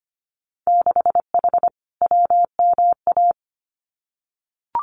Morse code, heard as a keyed tone, about 25 wpm, 690 Hz. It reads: 6H WMA
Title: Winter Field Day Exchanges Encore 25wpm